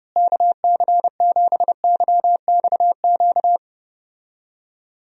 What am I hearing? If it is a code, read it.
Morse code, heard as a keyed tone, about 29 wpm, 700 Hz. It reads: KC7YXQ
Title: USA Call Signs - 6 Characters - Repeated 3x - 30wpm